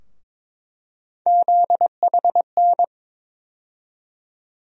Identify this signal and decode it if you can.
Morse code, heard as a keyed tone, about 21 wpm, 700 Hz. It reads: ZHN
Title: USA Call Sign Postfixes - 3 Letters 22wpm